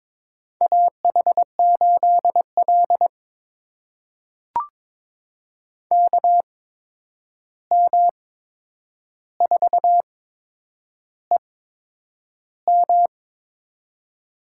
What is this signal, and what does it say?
Morse code, heard as a keyed tone, about 22 wpm, spaced as for 9 wpm, 700 Hz. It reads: AH8L KM4EM
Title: USA Call Signs - 8x Character Spacing - 22wpm